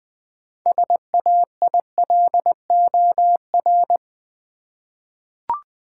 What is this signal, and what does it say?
Morse code, heard as a keyed tone, about 20 wpm, 690 Hz. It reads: SAILOR
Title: 6 Letter Words - First 3 Letters Given - 20wpm